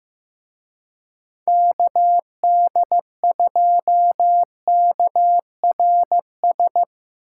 Morse code, heard as a keyed tone, about 17 wpm, 700 Hz. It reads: KD2KRS
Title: Speed Racing - USA 2x3 Call Signs 15wpm